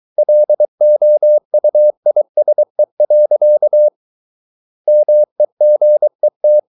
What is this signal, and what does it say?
Morse code, heard as a keyed tone, about 23 wpm, 600 Hz. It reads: LOUISE. MEGET
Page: Ferie 23 wpm